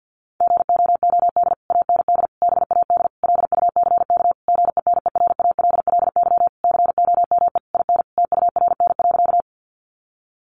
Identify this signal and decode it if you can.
Morse code, heard as a keyed tone, about 50 wpm, 700 Hz. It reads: GOOD AND BAD LUCK GENERALLY COME IN TURN.
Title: Sentences from Top 1500 Words 50wpm